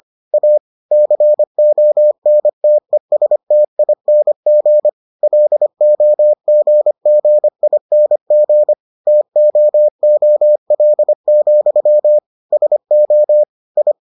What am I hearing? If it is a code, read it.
Morse code, heard as a keyed tone, about 25 wpm, 600 Hz. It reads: A CONTESTING LOGGING TOOL, SO I